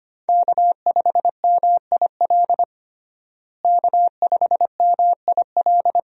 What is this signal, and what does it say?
Morse code, heard as a keyed tone, about 25 wpm, 700 Hz. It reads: K5MIL K5MIL
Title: USA Call Signs - 5 Characters - Repeated 2x - 25wpm